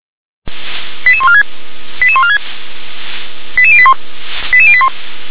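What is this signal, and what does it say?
Five-tone selective call (ZVEI), heard as quick runs of five tones, tones of about 70 ms. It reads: 88146 88146 88081 88081
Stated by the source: ZVEI five-tone selective call 88853, received via SDR.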